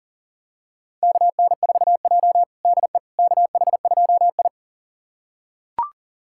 Morse code, heard as a keyed tone, about 38 wpm, 700 Hz. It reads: KN4J DE KH2I
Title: USA Callsign DE Callsign - 4 Characters - 40wpm